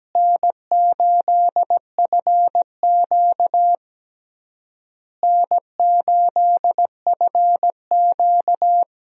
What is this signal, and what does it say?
Morse code, heard as a keyed tone, about 17 wpm, 700 Hz. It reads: N8FQ N8FQ
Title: USA Call Signs - 4 Characters - Repeated 3x - 17wpm